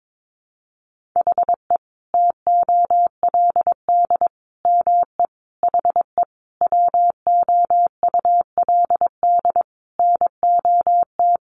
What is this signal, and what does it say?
Morse code, heard as a keyed tone, about 22 wpm, 700 Hz. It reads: HE TOLD ME HE WOULD NOT
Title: Sentences from Top 1300 Words 22wpm